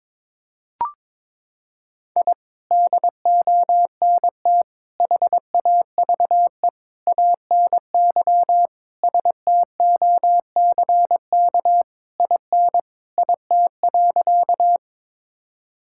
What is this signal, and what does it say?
Morse code, heard as a keyed tone, about 21 wpm, 700 Hz. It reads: I DONT HAVE ANY STOCK IN IT.
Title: Sentences from Top 1000 Words - Review Part 5 - 22wpm